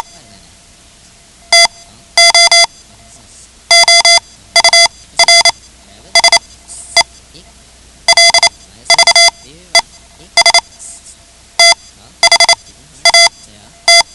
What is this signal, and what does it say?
Morse code, heard as a keyed tone, about 28 wpm, spaced as for 12 wpm, 2050 Hz. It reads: TO OURSELVES THAT
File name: ham radio / CodeQuick / Code Quick 10